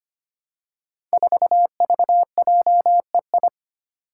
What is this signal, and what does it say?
Morse code, heard as a keyed tone, about 25 wpm, 700 Hz. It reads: 4VJEI
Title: Five Challenging Letter-Number Combinations 25wpm